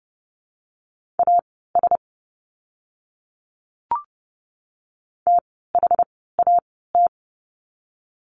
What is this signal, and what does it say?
Morse code, heard as a keyed tone, about 31 wpm, spaced as for 17 wpm, 700 Hz. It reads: AS THAT
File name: course / 30:10wpm / 115 Lesson 19f - Letter w - Words 30wpm